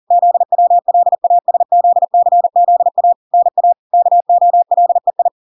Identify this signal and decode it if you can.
Morse code, heard as a keyed tone, about 40 wpm, 700 Hz. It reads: ZWLASZCZA NA KOLEI